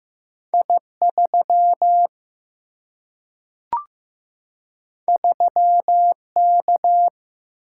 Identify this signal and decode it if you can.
Morse code, heard as a keyed tone, about 15 wpm, 700 Hz. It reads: I3 3K
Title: Single-Letter Single-Number 15wpm